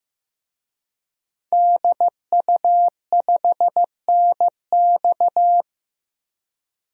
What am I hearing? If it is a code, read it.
Morse code, heard as a keyed tone, about 16 wpm, 700 Hz. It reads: DU5NX
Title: DXCC-100 Call Signs - 5 Characters - 15wpm